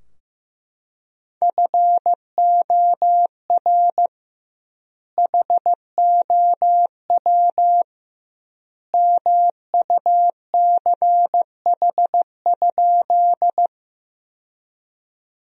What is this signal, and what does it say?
Morse code, heard as a keyed tone, about 15 wpm, 700 Hz. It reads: FOR HOW MUCH?
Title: 2x Word-Spacing - Sentences from Top 200 Words - 15wpm